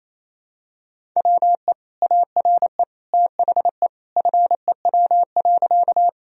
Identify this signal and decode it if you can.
Morse code, heard as a keyed tone, about 28 wpm, 700 Hz. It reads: WE ARE THE FEW.
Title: Sentences from Top 1000 Words - Review Part 1 - 28wpm